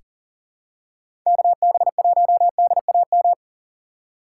Morse code, heard as a keyed tone, about 40 wpm, 700 Hz. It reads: KB1DAM
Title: USA Call Signs - 6 Characters - 4x Character Spacing - 40wpm